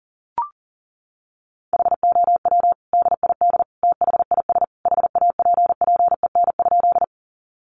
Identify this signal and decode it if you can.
Morse code, heard as a keyed tone, about 40 wpm, 700 Hz. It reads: HOW DID THIS HAPPEN?
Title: Sentences from Top 400 Words 40wpm